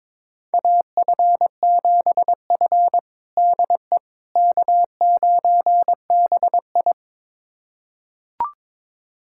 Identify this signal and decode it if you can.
Morse code, heard as a keyed tone, about 22 wpm, 700 Hz. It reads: AF7F DE K9BI
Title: USA Callsign DE Callsign - 4 Characters - 22wpm